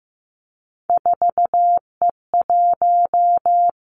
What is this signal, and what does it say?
Morse code, heard as a keyed tone, about 15 wpm, 700 Hz. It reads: 4E1
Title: Three Challenging Letter-Number Combinations 15wpm